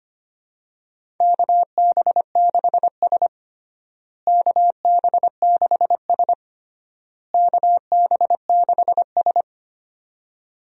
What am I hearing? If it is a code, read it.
Morse code, heard as a keyed tone, about 25 wpm, 700 Hz. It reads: KB6S KB6S KB6S
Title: USA Call Signs - 4 Characters - Repeated 3x - 25wpm